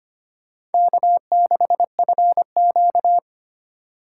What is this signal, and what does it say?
Morse code, heard as a keyed tone, about 32 wpm, 700 Hz. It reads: K6FQ